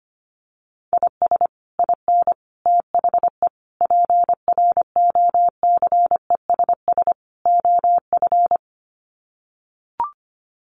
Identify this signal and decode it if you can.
Morse code, heard as a keyed tone, about 25 wpm, 700 Hz. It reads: IS IN THE PROCESS OF
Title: Sets of 5 Words Encore2 25wpm